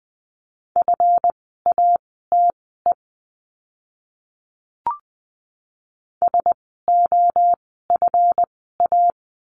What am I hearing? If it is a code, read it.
Morse code, heard as a keyed tone, about 20 wpm, spaced as for 14 wpm, 700 Hz. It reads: FATE SOFA
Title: 139 Lesson 23f - Letter f - Words 20wpm